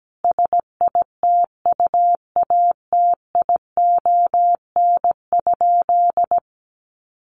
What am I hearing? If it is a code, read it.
Morse code, heard as a keyed tone, about 17 wpm, 700 Hz. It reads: SITUATION?
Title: Sentences from Top 1200 Words 17wpm